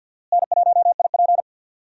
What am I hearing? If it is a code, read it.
Morse code, heard as a keyed tone, about 47 wpm, 700 Hz. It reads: N1IP
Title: USA Call Signs - 4 Characters - Repeated 3x - 50wpm